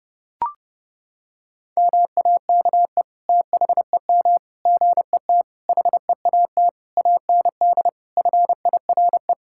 Morse code, heard as a keyed tone, about 30 wpm, 700 Hz. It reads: MAKE THEM GET HEAT AND FIRE
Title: Sentences from Top 1600 Words 30wpm